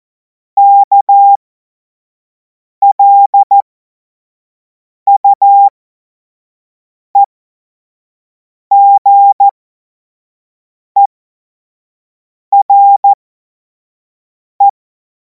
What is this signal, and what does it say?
Morse code, heard as a keyed tone, about 14 wpm, spaced as for 5 wpm, 800 Hz. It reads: KLUEGERE
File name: Morsekurs / Witze / Witz 19. Mai 2026 25 F70